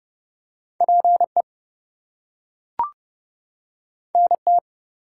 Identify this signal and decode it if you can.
Morse code, heard as a keyed tone, about 30 wpm, 700 Hz. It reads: PE NT
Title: US States and Canadian Provinces 30wpm